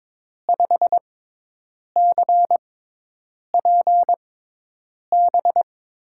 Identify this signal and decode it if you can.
Morse code, heard as a keyed tone, about 22 wpm, spaced as for 8 wpm, 700 Hz. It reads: 5CPB
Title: USA Call Signs - 6x Character Spacing - 22wpm